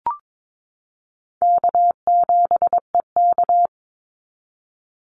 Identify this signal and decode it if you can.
Morse code, heard as a keyed tone, about 22 wpm, 700 Hz. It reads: K7EK
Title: Easy Call Signs Repeated 2x - 22wpm